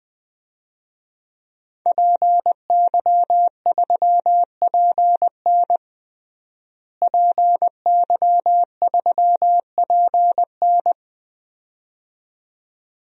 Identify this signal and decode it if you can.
Morse code, heard as a keyed tone, about 20 wpm, 700 Hz. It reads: PY3PN PY3PN
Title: DXCC-100 Call Signs - 5 Characters - Repeated 2x - 20wpm